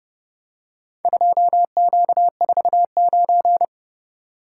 Morse code, heard as a keyed tone, about 29 wpm, 700 Hz. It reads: 2Q49